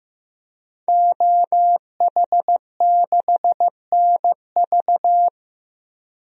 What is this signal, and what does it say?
Morse code, heard as a keyed tone, about 15 wpm, 700 Hz. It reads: OH6NV
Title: DXCC-100 Call Signs - 5 Characters - 15wpm